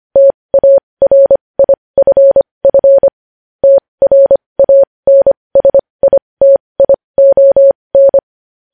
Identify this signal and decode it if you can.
Morse code, heard as a keyed tone, about 25 wpm, spaced as for 20 wpm, 550 Hz. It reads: TARIFF TRANSITION
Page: News Headlines in Morse Code at 20 WPM